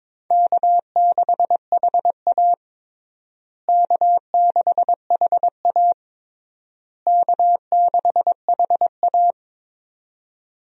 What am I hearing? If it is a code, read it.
Morse code, heard as a keyed tone, about 22 wpm, 700 Hz. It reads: K6HA K6HA K6HA
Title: USA Call Signs - 4 Characters - Repeated 3x - 22wpm